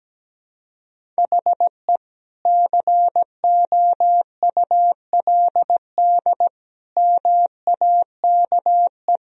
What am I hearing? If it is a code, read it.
Morse code, heard as a keyed tone, about 17 wpm, 700 Hz. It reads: HE COULD MAKE
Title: Sentences from Top 1000 Words - Review Part 4 - 17wpm